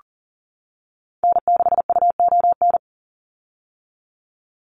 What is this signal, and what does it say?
Morse code, heard as a keyed tone, about 40 wpm, 690 Hz. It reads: N6UON
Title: USA Call Signs 40wpm